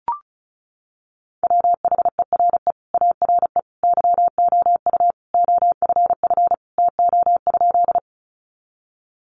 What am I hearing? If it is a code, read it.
Morse code, heard as a keyed tone, about 34 wpm, 700 Hz. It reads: WHERE ARE YOU OFF TO?
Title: Sentences from Top 200 Words 35wpm